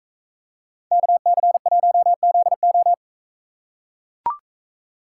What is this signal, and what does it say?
Morse code, heard as a keyed tone, about 43 wpm, 700 Hz. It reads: KC1ZO